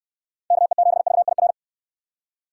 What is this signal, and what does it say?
Morse code, heard as a keyed tone, about 66 wpm, 700 Hz. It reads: KE7VIR